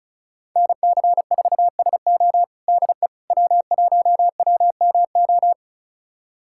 Morse code, heard as a keyed tone, about 36 wpm, 700 Hz. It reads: NC4SO DE W1WMO
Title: USA Callsign DE Callsign - 5 Characters - 35wpm